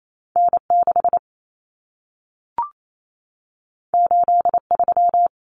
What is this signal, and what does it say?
Morse code, heard as a keyed tone, about 28 wpm, 700 Hz. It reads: N6 83
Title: Common Cut Numbers - 2-Digit 28wpm